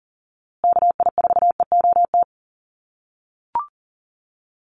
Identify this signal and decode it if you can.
Morse code, heard as a keyed tone, about 39 wpm, 700 Hz. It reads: KI4EOT
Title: USA Call Signs Repeated 3x 40wpm